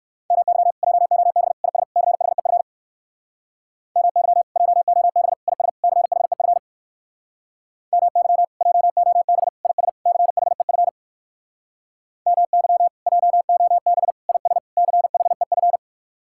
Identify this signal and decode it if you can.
Morse code, heard as a keyed tone, about 58 wpm, 700 Hz. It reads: MY JOB IS CHEF MY JOB IS CHEF MY JOB IS CHEF MY JOB IS CHEF